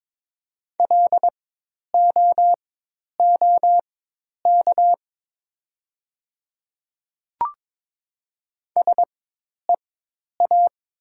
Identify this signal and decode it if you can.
Morse code, heard as a keyed tone, about 22 wpm, spaced as for 10 wpm, 700 Hz. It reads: LOOK SEA
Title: 4x Character Spacing - Top 500 Words - 22wpm